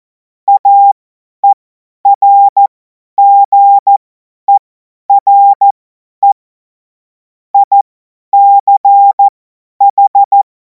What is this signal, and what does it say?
Morse code, heard as a keyed tone, about 14 wpm, spaced as for 10 wpm, 800 Hz. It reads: AERGERE ICH
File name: Morsekurs / Witze / Witz 17. März 2026 50 F70